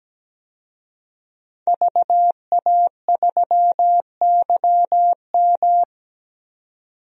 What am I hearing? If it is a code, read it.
Morse code, heard as a keyed tone, about 17 wpm, 700 Hz. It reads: VA3YM
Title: DXCC-100 Call Signs - 5 Characters - 17wpm